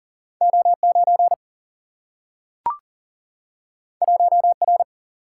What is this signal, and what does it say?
Morse code, heard as a keyed tone, about 40 wpm, 700 Hz. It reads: O9 1R
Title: Single-Letter Single-Number 40wpm